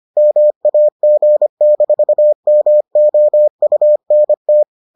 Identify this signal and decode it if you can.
Morse code, heard as a keyed tone, about 25 wpm, 600 Hz. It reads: MAG-MOUNT